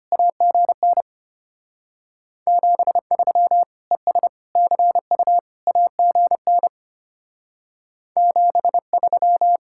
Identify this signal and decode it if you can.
Morse code, heard as a keyed tone, about 33 wpm, 700 Hz. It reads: AGN 73 ES CU AGN 73